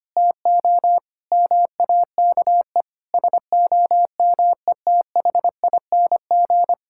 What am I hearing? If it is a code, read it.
Morse code, heard as a keyed tone, about 25 wpm, 700 Hz. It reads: TO MAKE SOMETHING
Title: Sentences from Top 1000 Words - Review Part 5 - 25wpm